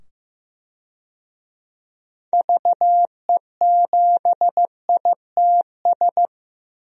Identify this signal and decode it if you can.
Morse code, heard as a keyed tone, about 15 wpm, 700 Hz. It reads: VE7ITS
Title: DXCC-100 Call Signs - 6 Characters - 15wpm